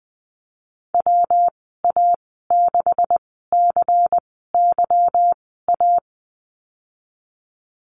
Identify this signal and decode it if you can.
Morse code, heard as a keyed tone, about 19 wpm, spaced as for 14 wpm, 700 Hz. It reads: WA6CYA
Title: 161 Lesson 26g - Letter g - Call Signs 20wpm